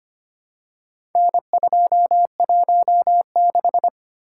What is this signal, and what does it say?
Morse code, heard as a keyed tone, about 25 wpm, 700 Hz. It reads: N216
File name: practice / 25wpm / Four Letter-Number Combinations - 25wpm